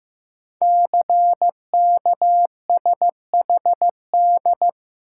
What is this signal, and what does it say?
Morse code, heard as a keyed tone, about 15 wpm, 700 Hz. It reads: CKSHD
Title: Code Groups - 5 Letters - 15wpm